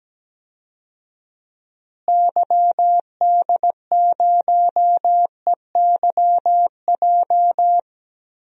Morse code, heard as a keyed tone, about 17 wpm, 700 Hz. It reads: YD0EYJ
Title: DXCC-100 Call Signs - 6 Characters - Repeated 2x - 17wpm